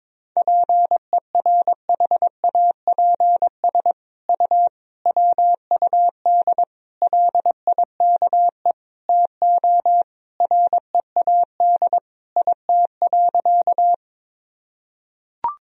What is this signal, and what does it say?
Morse code, heard as a keyed tone, about 22 wpm, 700 Hz. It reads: PERHAPS U WUD LIKE TO READ IT.
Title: Sentences from Top 500 Words and Abbreviations 22wpm